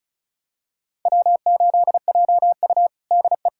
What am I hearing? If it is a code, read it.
Morse code, heard as a keyed tone, about 36 wpm, 700 Hz. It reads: W8JU DE
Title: USA Callsign DE Callsign - 4 Characters - 35wpm